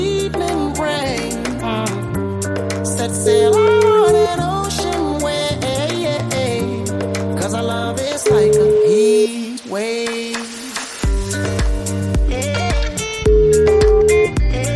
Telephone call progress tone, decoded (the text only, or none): ringback tone